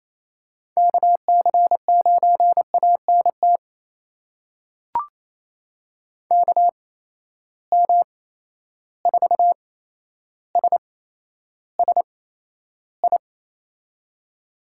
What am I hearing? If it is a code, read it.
Morse code, heard as a keyed tone, about 28 wpm, spaced as for 12 wpm, 700 Hz. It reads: KC9ANT KM4SSI
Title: USA Call Signs - 6 Characters - 8x Character Spacing - 28wpm